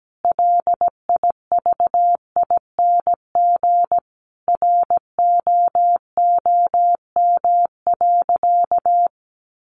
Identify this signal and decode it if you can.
Morse code, heard as a keyed tone, about 17 wpm, 700 Hz. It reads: LIVING ROOM.
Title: Sentences from Top 1200 Words 17wpm